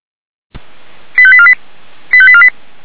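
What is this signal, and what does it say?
Five-tone selective call (ZVEI), heard as quick runs of five tones, tones of about 70 ms. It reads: 85848 85848
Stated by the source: ZVEI five-tone selective call received on an SDR.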